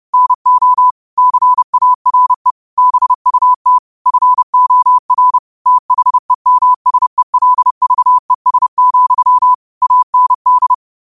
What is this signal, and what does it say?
Morse code, heard as a keyed tone, about 30 wpm, 1000 Hz. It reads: NO CARE BUT FOR THEMSELVES, AND